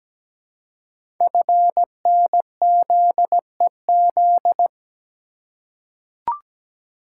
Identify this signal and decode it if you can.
Morse code, heard as a keyed tone, about 17 wpm, 700 Hz. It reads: FNZEZ
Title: Code Groups - 5 Letters - 17wpm